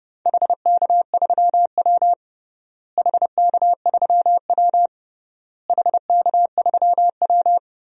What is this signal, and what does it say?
Morse code, heard as a keyed tone, about 32 wpm, 700 Hz. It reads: HK3W HK3W HK3W
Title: DXCC-100 Call Signs - 4 Characters - Repeated 3x - 30wpm